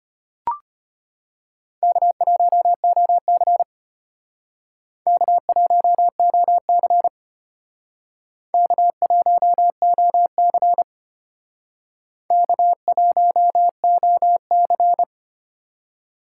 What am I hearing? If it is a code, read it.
Morse code, heard as a keyed tone, about 34 wpm, 700 Hz. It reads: K1OC K1OC K1OC K1OC